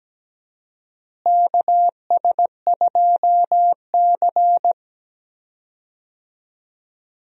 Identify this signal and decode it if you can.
Morse code, heard as a keyed tone, about 17 wpm, 700 Hz. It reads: KS2C
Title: USA Call Signs - 4 Characters - Repeated 2x - 17wpm